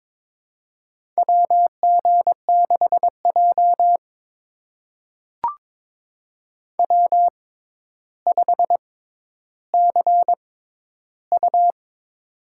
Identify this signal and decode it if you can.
Morse code, heard as a keyed tone, about 21 wpm, spaced as for 11 wpm, 700 Hz. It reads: WG6J W5CU
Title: USA Call Signs - 4 Characters - 6x Character Spacing - 22wpm